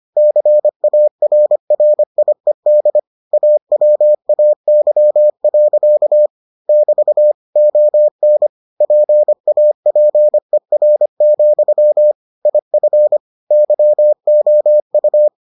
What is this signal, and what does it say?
Morse code, heard as a keyed tone, about 25 wpm, 600 Hz. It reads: CARRIED AWAY. <BT> ON PAPER, IF YOU